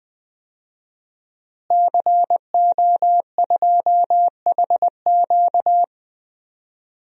Morse code, heard as a keyed tone, about 20 wpm, 700 Hz. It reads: CO2HQ
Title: DXCC-100 Call Signs - 5 Characters - Repeated 2x - 20wpm